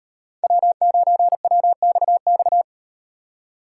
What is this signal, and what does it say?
Morse code, heard as a keyed tone, about 36 wpm, 700 Hz. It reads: W9WXX